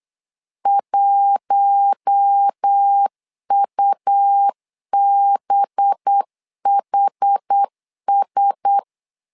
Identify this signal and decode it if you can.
Morse code, heard as a keyed tone, about 8 wpm, 800 Hz. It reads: 1UBHS
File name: LEZIONI / 5 LBVC / 5 LBV 8wpm 2837